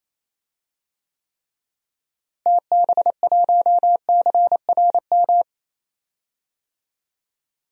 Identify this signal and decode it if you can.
Morse code, heard as a keyed tone, about 27 wpm, 700 Hz. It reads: TB1CRM
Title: DXCC-100 Call Signs - 6 Characters - Repeated 3x - 28wpm